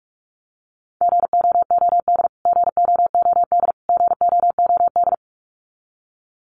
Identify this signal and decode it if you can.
Morse code, heard as a keyed tone, about 45 wpm, 700 Hz. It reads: GOOD GOOD GOOD
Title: Top 200 Words Repeated 45wpm